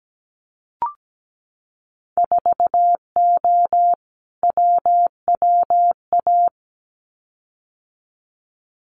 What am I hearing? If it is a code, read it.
Morse code, heard as a keyed tone, about 17 wpm, 700 Hz. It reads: 4O WWA
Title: Winter Field Day Exchanges 17wpm